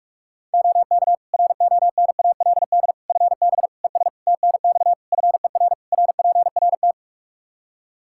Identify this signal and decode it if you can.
Morse code, heard as a keyed tone, about 46 wpm, 700 Hz. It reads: OK RONALD FB ES TNX FER RPRT
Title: Practice QSO - Exchange 2 - Info - 45wpm